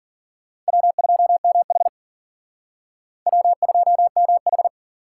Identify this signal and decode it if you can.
Morse code, heard as a keyed tone, about 54 wpm, 700 Hz. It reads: W2MH W2MH